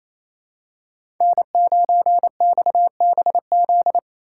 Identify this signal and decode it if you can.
Morse code, heard as a keyed tone, about 28 wpm, 700 Hz. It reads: N9XBZ
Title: USA Call Signs - 3x Character Spacing - 28wpm